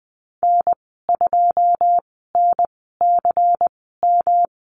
Morse code, heard as a keyed tone, about 20 wpm, spaced as for 15 wpm, 700 Hz. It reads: N2NCM
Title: 122 Lesson 20g - Number 3 - Call Signs 20wpm